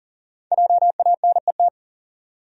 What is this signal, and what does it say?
Morse code, heard as a keyed tone, about 40 wpm, 700 Hz. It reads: JANET
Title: USA Common Names - Max 5 Letters - 40wpm